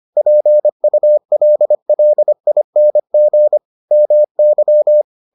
Morse code, heard as a keyed tone, about 25 wpm, 600 Hz. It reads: PULLING MY